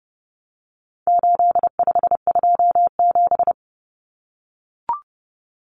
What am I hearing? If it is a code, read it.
Morse code, heard as a keyed tone, about 30 wpm, 700 Hz. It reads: 8527
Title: Common Cut Numbers - 4-Digit 30wpm